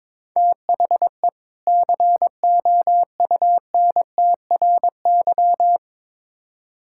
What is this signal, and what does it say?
Morse code, heard as a keyed tone, about 22 wpm, 700 Hz. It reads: THE COUNTRY
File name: Build A Phrase - 2-Word Phrases - 22wpm